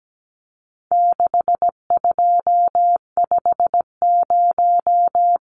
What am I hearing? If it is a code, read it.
Morse code, heard as a keyed tone, about 17 wpm, 700 Hz. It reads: 6250
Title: Numbers - 4-Digit 17wpm